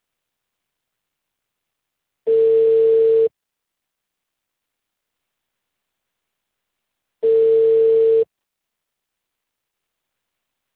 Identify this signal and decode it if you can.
Ringback tone (China, 450 Hz).